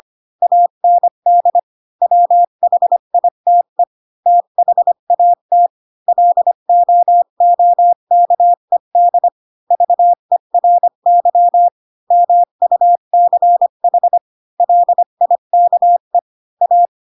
Morse code, heard as a keyed tone, about 25 wpm, 700 Hz. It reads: AND WHITE THAT LOOKED VERY MUCH LIKE A
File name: Pinocchio / Pinocchio ch5 20 25 ik1hge